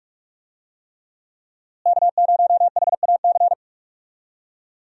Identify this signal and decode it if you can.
Morse code, heard as a keyed tone, about 42 wpm, 700 Hz. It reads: K0HAC
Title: USA Call Signs - 5 Characters - Repeated 3x - 45wpm